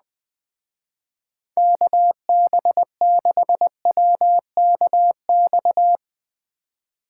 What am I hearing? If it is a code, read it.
Morse code, heard as a keyed tone, about 20 wpm, 700 Hz. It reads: KB6WKX
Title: Speed Racing - USA 2x3 Call Signs 17wpm